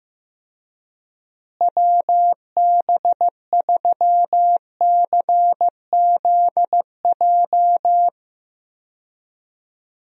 Morse code, heard as a keyed tone, about 15 wpm, 700 Hz. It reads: WB3CZJ
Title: USA Call Signs - 6 Characters - Repeated 2x - 15wpm